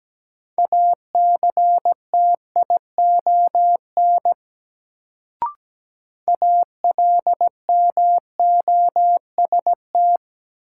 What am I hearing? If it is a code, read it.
Morse code, heard as a keyed tone, about 17 wpm, 700 Hz. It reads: ACTION ALMOST
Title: Sentences from Top 700 Words - New Word Drill - 17wpm